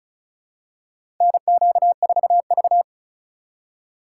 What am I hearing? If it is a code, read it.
Morse code, heard as a keyed tone, about 36 wpm, 700 Hz. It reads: NQ4V
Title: USA Call Signs - 4x Character Spacing - 35wpm